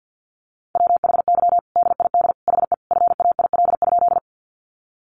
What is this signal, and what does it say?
Morse code, heard as a keyed tone, about 50 wpm, 700 Hz. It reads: WHY DID HE FAIL?
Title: Sentences from Top 1100 Words 50wpm